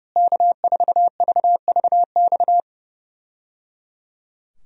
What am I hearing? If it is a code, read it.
Morse code, heard as a keyed tone, about 30 wpm, 700 Hz. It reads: K4VVX
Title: USA Call Signs - 5 Characters - Repeated 3x - 30wpm